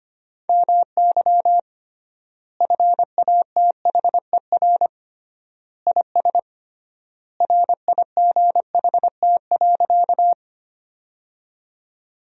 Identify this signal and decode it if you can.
Morse code, heard as a keyed tone, about 25 wpm, 700 Hz. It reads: MY FATHER IS RIGHT.
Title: 3x Word-Spacing - Sentences from Top 300 Words - 25wpm